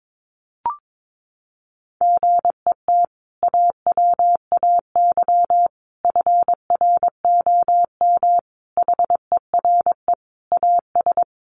Morse code, heard as a keyed tone, about 22 wpm, 700 Hz. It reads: GET AWAY FROM HERE AS
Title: Sentences from Top 600 Words 22wpm